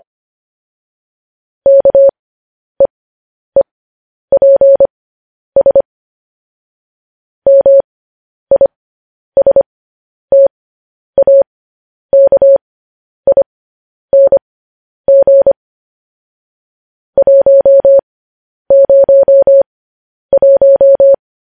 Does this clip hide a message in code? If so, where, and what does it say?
Yes: Morse code from 1.7 to 21.1 s, heard as a keyed tone, about 25 wpm, spaced as for 10 wpm, 550 Hz. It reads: KEEPS MISTAKING 101